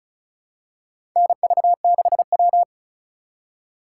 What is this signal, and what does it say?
Morse code, heard as a keyed tone, about 36 wpm, 700 Hz. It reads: NV6W
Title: USA Call Signs - 8x Character Spacing - 35wpm